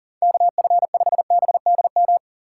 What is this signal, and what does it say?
Morse code, heard as a keyed tone, about 40 wpm, 700 Hz. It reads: KF5BDM
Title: USA Call Signs - 6 Characters - 6x Character Spacing - 40wpm